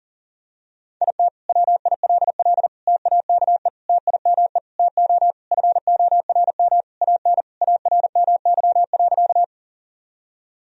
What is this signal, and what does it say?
Morse code, heard as a keyed tone, about 40 wpm, 700 Hz. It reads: IT WILL TAKE TIME TO FORM AN ARMY.
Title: Sentences from Top 1000 Words - Review Part 1 - 40wpm